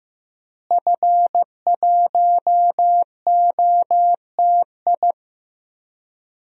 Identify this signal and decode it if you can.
Morse code, heard as a keyed tone, about 15 wpm, 700 Hz. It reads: F1OTI
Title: DXCC-100 Call Signs - 5 Characters - Repeated 2x - 15wpm